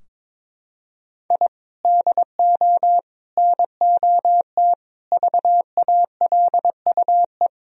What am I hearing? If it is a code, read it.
Morse code, heard as a keyed tone, about 21 wpm, 700 Hz. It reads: I DO NOT VALUE
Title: Sentences from Top 1000 Words - Review Part 4 - 22wpm